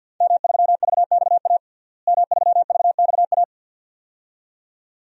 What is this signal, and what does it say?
Morse code, heard as a keyed tone, about 53 wpm, 700 Hz. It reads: M3VXA M3VXA
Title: DXCC-100 Call Signs - 5 Characters - Repeated 2x - 50wpm